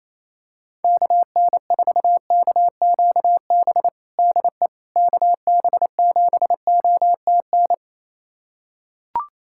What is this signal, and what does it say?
Morse code, heard as a keyed tone, about 27 wpm, 700 Hz. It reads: KN4KQB DE KB7OTN
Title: USA Callsign DE Callsign 28wpm